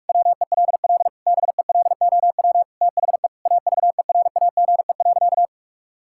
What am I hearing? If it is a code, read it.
Morse code, heard as a keyed tone, about 41 wpm, 700 Hz. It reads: WELL BELOW THE AVERAGE.
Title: Sentences from Top 1000 Words - Review Part 6 - 45wpm